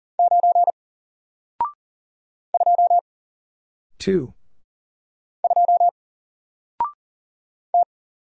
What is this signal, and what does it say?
Morse code, heard as a keyed tone, about 40 wpm, 700 Hz. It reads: 9 2 2 T
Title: Single Character (Letter:Number:Punctuation:Prosign) Alt 40wpm